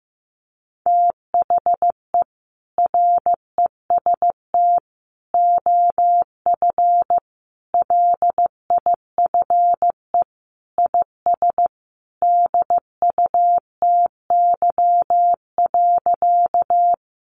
Morse code, heard as a keyed tone, about 15 wpm, 700 Hz. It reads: THE REST OF LIFE IS DUTY.
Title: Sentences from Top 1500 Words 15wpm